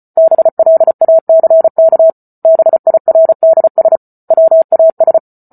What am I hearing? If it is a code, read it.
Morse code, heard as a keyed tone, about 34 wpm, 650 Hz. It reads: BLACK BIRDS WAS